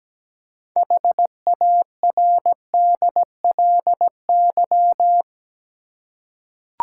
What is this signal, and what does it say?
Morse code, heard as a keyed tone, about 17 wpm, 700 Hz. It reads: HARDLY
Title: Six Letter Words - 17wpm